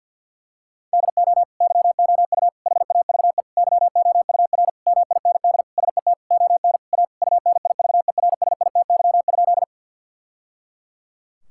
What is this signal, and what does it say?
Morse code, heard as a keyed tone, about 50 wpm, 700 Hz. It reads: DO YOU HAVE YOUR MIND SET ON A UNIVERSITY?
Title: Sentences from Top 1000 Words - Review Part 5 - 50wpm